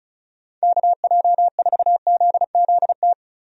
Morse code, heard as a keyed tone, about 34 wpm, 700 Hz. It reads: KJ4ZZT
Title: USA Call Signs - 6 Characters - 35wpm